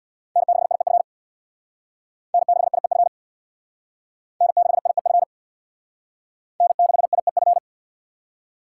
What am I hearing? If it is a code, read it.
Morse code, heard as a keyed tone, about 67 wpm, 700 Hz. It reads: N6IEF N6IEF N6IEF N6IEF